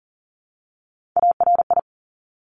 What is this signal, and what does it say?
Morse code, heard as a keyed tone, about 40 wpm, 700 Hz. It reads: ARI
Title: Top Three Letter Combinations in Words 40wpm